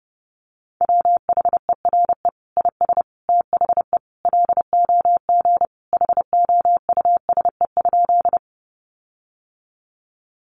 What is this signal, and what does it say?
Morse code, heard as a keyed tone, about 30 wpm, 700 Hz. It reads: WHERE IS THE LOG HOUSE?
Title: Sentences from Top 1400 Words 30wpm